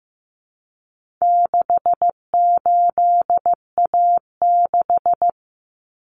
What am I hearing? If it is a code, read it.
Morse code, heard as a keyed tone, about 15 wpm, 700 Hz. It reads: 68A6
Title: Common Cut Numbers - 4-Digit 15wpm